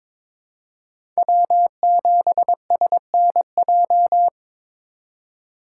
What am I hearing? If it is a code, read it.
Morse code, heard as a keyed tone, about 23 wpm, 700 Hz. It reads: W7SNJ
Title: USA Call Signs - 5 Characters - 3x Character Spacing - 22wpm